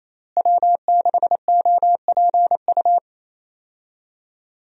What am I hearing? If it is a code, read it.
Morse code, heard as a keyed tone, about 27 wpm, 700 Hz. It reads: W6OPU
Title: USA Call Signs - 5 Characters - Repeated 2x - 28wpm